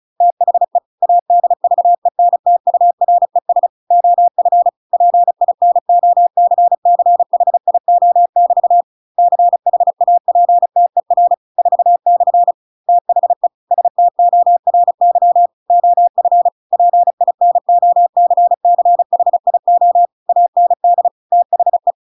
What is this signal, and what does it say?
Morse code, heard as a keyed tone, about 34 wpm, 700 Hz. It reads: THE ADVENTURES OF PINOCCHIO<BT> CHAPTER 4/ THE STORY OF PINOCCHIO AND THE